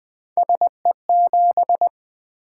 Morse code, heard as a keyed tone, about 20 wpm, 700 Hz. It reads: SE7
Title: Three Letter-Number Combinations - 20wpm